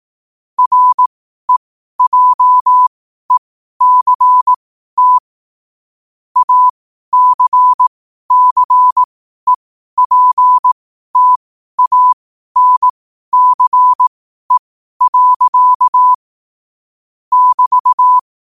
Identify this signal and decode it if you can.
Morse code, heard as a keyed tone, about 18 wpm, spaced as for 12 wpm, 1000 Hz. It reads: REJECT ACCEPTANCE. <BT>
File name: qotdfiles / 12wpm